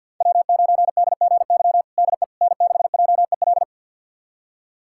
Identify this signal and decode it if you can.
Morse code, heard as a keyed tone, about 52 wpm, 700 Hz. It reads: W9DGY DE N6JEL
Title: USA Callsign DE Callsign - 5 Characters - 50wpm